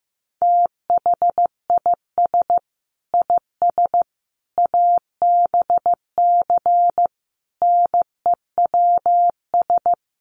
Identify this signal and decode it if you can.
Morse code, heard as a keyed tone, about 14 wpm, 700 Hz. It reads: THIS IS ABC NEWS
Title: Sets of 5 Words 15wpm